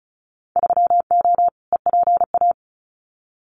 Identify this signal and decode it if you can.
Morse code, heard as a keyed tone, about 36 wpm, 700 Hz. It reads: 3O EPA
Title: Winter Field Day Exchanges Encore 35wpm